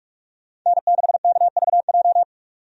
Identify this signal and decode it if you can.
Morse code, heard as a keyed tone, about 46 wpm, 700 Hz. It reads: N6KVJ
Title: USA Call Signs - 5 Characters - 45wpm